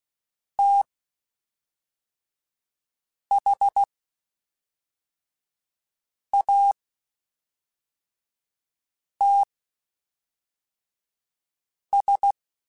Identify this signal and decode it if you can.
Morse code, heard as a keyed tone, about 16 wpm, spaced as for 3 wpm, 800 Hz. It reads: THATS
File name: (spaced) / QSOs Spaced-mp3s / WD0FBT 1 @5 sp